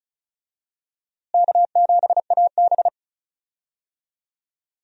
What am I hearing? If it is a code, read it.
Morse code, heard as a keyed tone, about 36 wpm, 700 Hz. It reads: K7AB
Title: USA Call Signs - 4 Characters - Repeated 2x - 35wpm